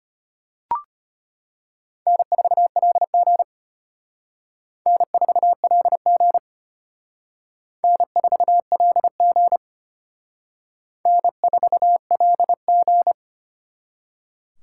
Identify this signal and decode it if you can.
Morse code, heard as a keyed tone, about 32 wpm, 700 Hz. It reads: N4LG N4LG N4LG N4LG